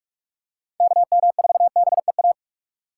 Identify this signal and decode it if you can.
Morse code, heard as a keyed tone, about 46 wpm, 700 Hz. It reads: KM4BEA
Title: USA Call Signs - 3x Character Spacing - 45wpm